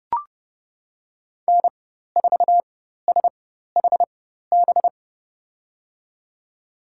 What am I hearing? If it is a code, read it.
Morse code, heard as a keyed tone, about 32 wpm, spaced as for 14 wpm, 700 Hz. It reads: N4SHB
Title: USA Call Signs - 4x Character Spacing - 30wpm